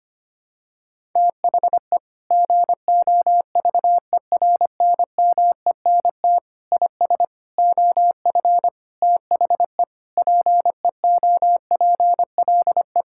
Morse code, heard as a keyed tone, about 25 wpm, 700 Hz. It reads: THE GOVERNMENT IS OF THE PEOPLE
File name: practice / 25wpm / Sentences with 9-Word Phrases - 25wpm